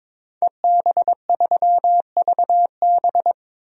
Morse code, heard as a keyed tone, about 22 wpm, 700 Hz. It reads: EB3VB
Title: DXCC-100 Call Signs - 5 Characters - 22wpm